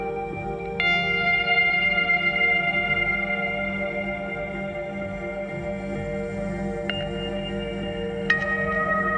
Dial-up modem handshake.